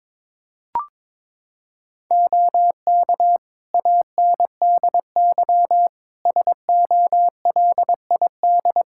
Morse code, heard as a keyed tone, about 22 wpm, 700 Hz. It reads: OK ANDY SOLID
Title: K1USN SST QSO Practice 22wpm